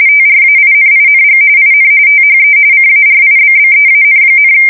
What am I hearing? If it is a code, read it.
RTTY, about 45.45 baud. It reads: TRAINING IN THE CARIBBEAN,
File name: fsk170 / battleshipnj